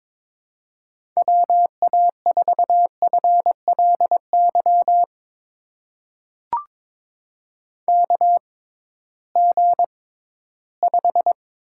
Morse code, heard as a keyed tone, about 23 wpm, spaced as for 14 wpm, 700 Hz. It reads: WA4FLY KG5
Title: USA Call Signs - 6 Characters - 6x Character Spacing - 22wpm